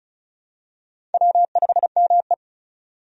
Morse code, heard as a keyed tone, about 36 wpm, 700 Hz. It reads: W5ME
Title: USA Call Signs - 4 Characters - 35wpm